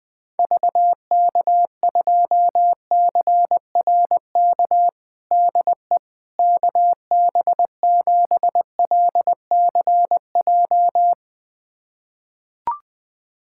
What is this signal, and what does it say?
Morse code, heard as a keyed tone, about 21 wpm, 700 Hz. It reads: VK2CRK DE KB7LCJ
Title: USA Callsign DE Callsign 20wpm